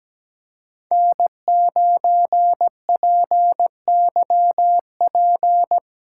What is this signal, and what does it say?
Morse code, heard as a keyed tone, about 19 wpm, 700 Hz. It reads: N9PYP
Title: Speed Racing - USA 1x3 Call Signs 17wpm